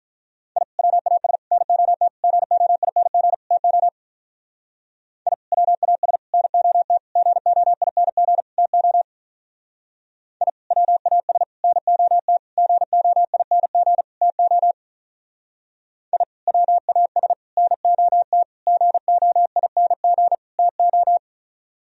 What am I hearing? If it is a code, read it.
Morse code, heard as a keyed tone, about 42 wpm, 700 Hz. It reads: I WAS NOT GOING TO I WAS NOT GOING TO I WAS NOT GOING TO I WAS NOT GOING TO
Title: Speed-Racing - Sets of 5 Words 35wpm